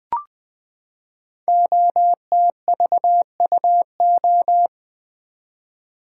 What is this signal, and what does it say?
Morse code, heard as a keyed tone, about 20 wpm, 700 Hz. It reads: OTVUO
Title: Code Groups - 5 Letter:Numbers - 20wpm